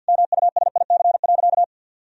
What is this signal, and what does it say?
Morse code, heard as a keyed tone, about 51 wpm, 700 Hz. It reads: MUSIC.
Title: Sentences from Top 1000 Words - Review Part 1 - 50wpm